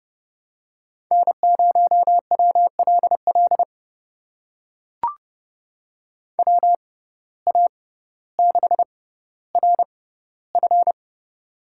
Morse code, heard as a keyed tone, about 29 wpm, spaced as for 15 wpm, 700 Hz. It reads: N0WLL WA6RF
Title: USA Call Signs - 5 Characters - 6x Character Spacing - 30wpm